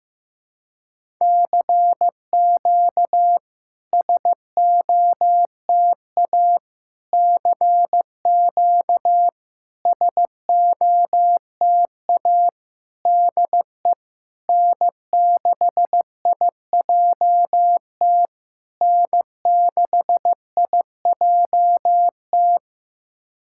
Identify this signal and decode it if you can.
Morse code, heard as a keyed tone, about 15 wpm, 700 Hz. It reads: CQ SOTA CQ SOTA DE N6IJT N6IJT
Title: SOTA QSO Practice 15wpm